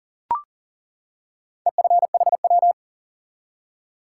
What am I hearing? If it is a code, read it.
Morse code, heard as a keyed tone, about 40 wpm, 700 Hz. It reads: EFHW
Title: Top 180 QSO Elements 40wpm